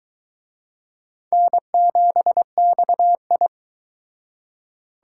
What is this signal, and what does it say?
Morse code, heard as a keyed tone, about 23 wpm, 700 Hz. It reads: N7XI